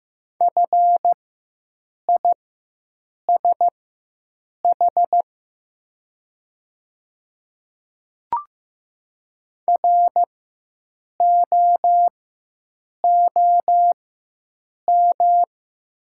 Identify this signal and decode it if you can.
Morse code, heard as a keyed tone, about 15 wpm, spaced as for 7 wpm, 700 Hz. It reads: FISH ROOM
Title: 4x Character Spacing - Top 300 Words - 15wpm